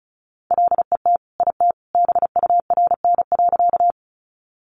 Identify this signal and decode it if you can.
Morse code, heard as a keyed tone, about 34 wpm, 700 Hz. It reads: LET IT BURN.
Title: Sentences from Top 1300 Words 35wpm